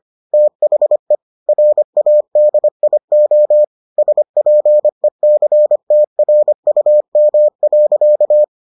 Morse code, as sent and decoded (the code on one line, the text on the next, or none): - .... / .-. .- -.. .. --- / ... .--. -.-. - .-. ..- -- .-.-.-
THE RADIO SPECTRUM.